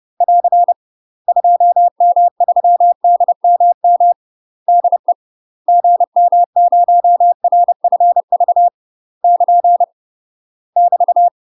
Morse code, heard as a keyed tone, about 30 wpm, 700 Hz. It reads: <AR> 2M3DMM DE GM0RFV <KN> <BT>
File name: UK MASTER QSOs TRAINING / 30WPM / UK MASTER QSOs 0011